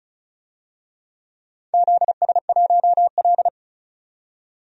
Morse code, heard as a keyed tone, about 34 wpm, 700 Hz. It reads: ZS1L
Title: DXCC-100 Call Signs - 4 Characters - 35wpm